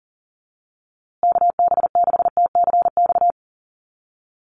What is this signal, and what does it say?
Morse code, heard as a keyed tone, about 40 wpm, 700 Hz. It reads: KB6TCX
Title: USA Call Signs Repeated 2x 40wpm